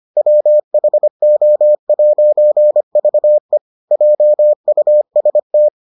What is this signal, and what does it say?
Morse code, heard as a keyed tone, about 25 wpm, 600 Hz. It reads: WHO'VE JUST